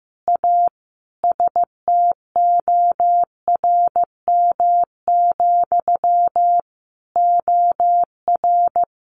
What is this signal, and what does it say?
Morse code, heard as a keyed tone, about 15 wpm, 700 Hz. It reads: A STORM, OR
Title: Sentences from Top 1500 Words 15wpm